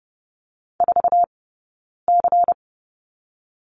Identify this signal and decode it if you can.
Morse code, heard as a keyed tone, about 30 wpm, 700 Hz. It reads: 4 C
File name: course / 30:10wpm / 099 Lesson 17d - Letter u - Two Characters 30wpm